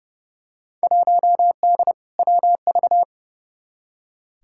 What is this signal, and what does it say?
Morse code, heard as a keyed tone, about 30 wpm, 700 Hz. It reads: 1D WV
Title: Field Day Exchanges - 30wpm